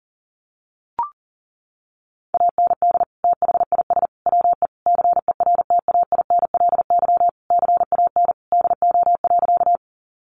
Morse code, heard as a keyed tone, about 37 wpm, 700 Hz. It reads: AND THIS WE CERTAINLY CAN DO.
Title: Sentences from Top 900 Words 40wpm